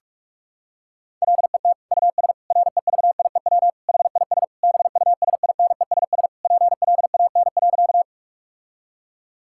Morse code, heard as a keyed tone, about 42 wpm, 700 Hz. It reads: LET US REVIEW HIS BUSINESS PLAN.
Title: Sentences from Top 1000 Words - Review Part 6 - 45wpm